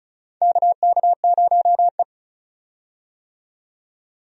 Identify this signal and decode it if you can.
Morse code, heard as a keyed tone, about 34 wpm, 700 Hz. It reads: KK0E
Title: USA Call Signs - 4 Characters - 35wpm